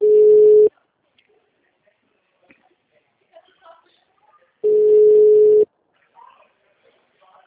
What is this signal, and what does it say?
Ringback tone (Europe, 425 Hz), with speech.
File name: out- -2004- - - .0